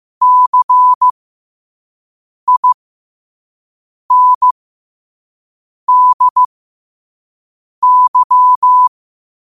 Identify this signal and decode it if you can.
Morse code, heard as a keyed tone, about 15 wpm, spaced as for 5 wpm, 1000 Hz. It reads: CINDY
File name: qotdfiles / 5wpm